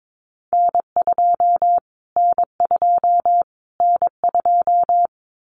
Morse code, heard as a keyed tone, about 22 wpm, 700 Hz. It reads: N2 N2 N2
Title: Callsign Prefixes for Top 25 DXCC - Repeated 3x - 22wpm